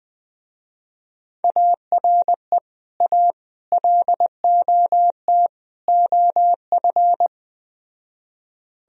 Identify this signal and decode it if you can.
Morse code, heard as a keyed tone, about 20 wpm, 700 Hz. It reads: ARE A LOT OF
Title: Speed-Racing - Sets of 4 Words 17wpm